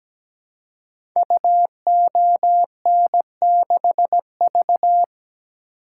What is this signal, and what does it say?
Morse code, heard as a keyed tone, about 17 wpm, 700 Hz. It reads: UON6V
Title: Code Groups - 5 Letter:Numbers - 17wpm